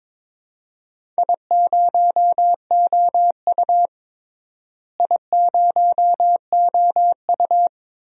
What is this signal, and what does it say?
Morse code, heard as a keyed tone, about 22 wpm, 700 Hz. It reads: I0OU I0OU
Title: DXCC-100 Call Signs - 4 Characters - Repeated 2x - 22wpm